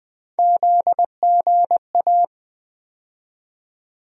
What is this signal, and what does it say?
Morse code, heard as a keyed tone, about 20 wpm, 700 Hz. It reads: ZGA
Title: USA Call Sign Postfixes - 3 Letters 20wpm